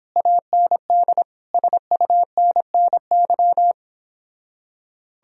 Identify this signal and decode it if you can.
Morse code, heard as a keyed tone, about 29 wpm, 700 Hz. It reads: AND SUNNY